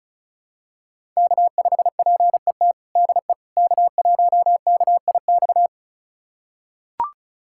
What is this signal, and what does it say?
Morse code, heard as a keyed tone, about 36 wpm, 700 Hz. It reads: K5PET DE K1KIX
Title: USA Callsign DE Callsign - 5 Characters - 35wpm